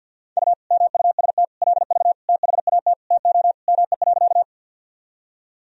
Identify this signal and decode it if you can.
Morse code, heard as a keyed tone, about 50 wpm, 700 Hz. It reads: U MUST LV THAT TO ME.
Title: Sentences from Top 400 Words and Abbreviations 50wpm